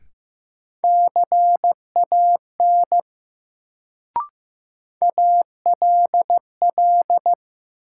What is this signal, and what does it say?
Morse code, heard as a keyed tone, about 15 wpm, 700 Hz. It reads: CAN ALL
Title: Sets of 2 Words - New Word Drill - 15wpm